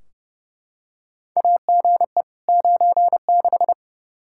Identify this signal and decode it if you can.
Morse code, heard as a keyed tone, about 30 wpm, 700 Hz. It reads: AGE 96
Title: Practice QSO - Exchange 3 - More Info - 30wpm